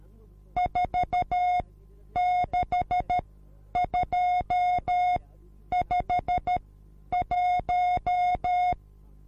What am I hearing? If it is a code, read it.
Morse code, heard as a keyed tone, about 13 wpm, spaced as for 9 wpm, 720 Hz. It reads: 46251